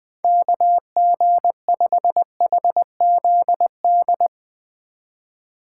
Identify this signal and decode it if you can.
Morse code, heard as a keyed tone, about 20 wpm, 700 Hz. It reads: KG5HZD
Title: USA Call Signs - 6 Characters - 2x Character Spacing - 20wpm